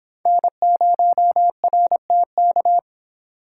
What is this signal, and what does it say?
Morse code, heard as a keyed tone, about 26 wpm, 700 Hz. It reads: N0RTK